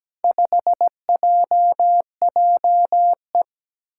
Morse code, heard as a keyed tone, about 17 wpm, 700 Hz. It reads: 5JJE
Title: Four Challenging Letter-Number Combinations 17wpm